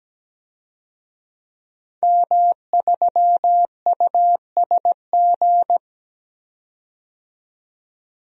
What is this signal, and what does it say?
Morse code, heard as a keyed tone, about 17 wpm, 700 Hz. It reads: M3USG
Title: DXCC-100 Call Signs - 5 Characters - 17wpm